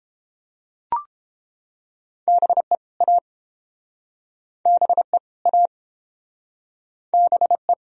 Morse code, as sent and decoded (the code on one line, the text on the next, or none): -... / .- / -... / .- / -... .
BE A BE A BE